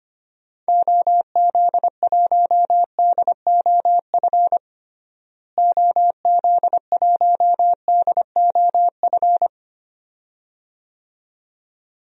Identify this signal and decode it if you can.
Morse code, heard as a keyed tone, about 25 wpm, 700 Hz. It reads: OZ1DOF OZ1DOF
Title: DXCC-100 Call Signs - 6 Characters - Repeated 2x - 25wpm